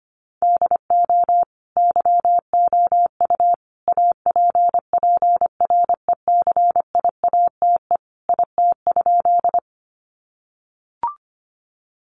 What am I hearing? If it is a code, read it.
Morse code, heard as a keyed tone, about 25 wpm, 700 Hz. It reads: DO YOU APPRECIATE IT?
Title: Sentences from Top 1300 Words 25wpm